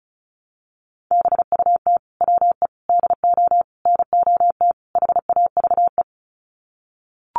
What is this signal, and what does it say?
Morse code, heard as a keyed tone, about 34 wpm, 700 Hz. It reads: BUT WE DO NOT HAVE
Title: Sets of 5 Words Encore2 35wpm